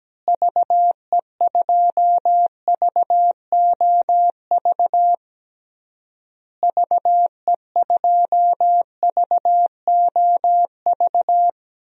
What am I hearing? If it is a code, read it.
Morse code, heard as a keyed tone, about 17 wpm, 700 Hz. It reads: VE2VOV VE2VOV
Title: DXCC-100 Call Signs - 6 Characters - Repeated 3x - 17wpm